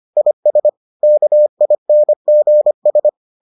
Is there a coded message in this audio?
Yes: Morse code from 0.2 to 3.1 s, heard as a keyed tone, about 25 wpm, 600 Hz. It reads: IS KINGS